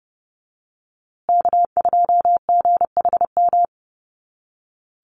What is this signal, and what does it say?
Morse code, heard as a keyed tone, about 30 wpm, 700 Hz. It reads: K2GHM
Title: USA Call Signs Repeated 3x 30wpm